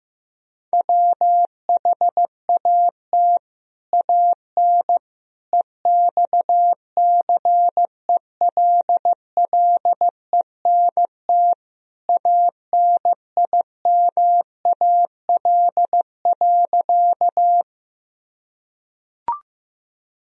Morse code, heard as a keyed tone, about 16 wpm, 700 Hz. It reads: WHAT AN EXCELLENT ANIMAL.
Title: Sentences from Top 1000 Words - Review Part 1 - 15wpm